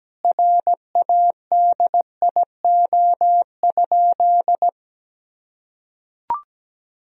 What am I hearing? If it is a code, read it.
Morse code, heard as a keyed tone, about 17 wpm, 700 Hz. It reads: RADIO?
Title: Sentences from Top 1000 Words - Review Part 3 - 17wpm